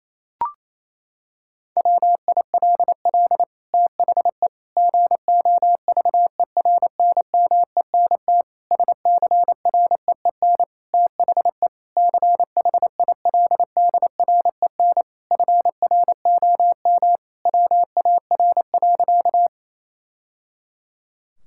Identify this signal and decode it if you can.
Morse code, heard as a keyed tone, about 27 wpm, 700 Hz. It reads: WILL THE GOVERNMENT SCREEN THE CHILDREN FROM WAR.
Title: Sentences from Top 1000 Words - Review Part 6 - 28wpm